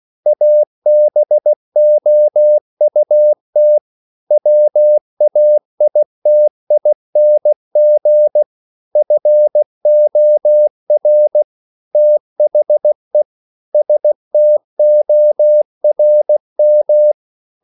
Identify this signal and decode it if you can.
Morse code, heard as a keyed tone, about 16 wpm, 600 Hz. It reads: ABOUT WAITING FOR THE STORM
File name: CW / Practice / EnglishQuotations0116